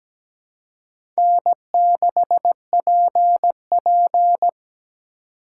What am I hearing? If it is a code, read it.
Morse code, heard as a keyed tone, about 17 wpm, 700 Hz. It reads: N6PP
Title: USA Call Signs - 4 Characters - 4x Character Spacing - 17wpm